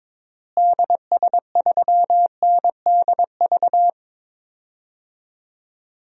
Morse code, heard as a keyed tone, about 22 wpm, 700 Hz. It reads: DS3NDV
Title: DXCC-100 Call Signs - 6 Characters - Repeated 2x - 22wpm